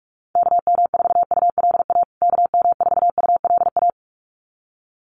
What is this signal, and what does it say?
Morse code, heard as a keyed tone, about 44 wpm, 700 Hz. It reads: KM4ULA KM4ULA
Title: USA Call Signs Repeated 2x 45wpm